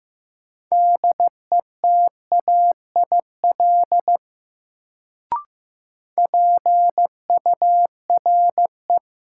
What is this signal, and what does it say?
Morse code, heard as a keyed tone, about 16 wpm, 700 Hz. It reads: DETAIL PURE
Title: Sentences from Top 1500 Words - New Word Drill Alt - 15wpm